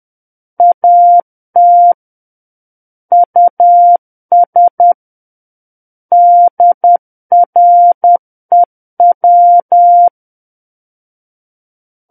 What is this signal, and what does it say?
Morse code, heard as a keyed tone, about 10 wpm, 700 Hz. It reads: AT US DREW
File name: Book One / 01-Chapter 1 - 10 WPM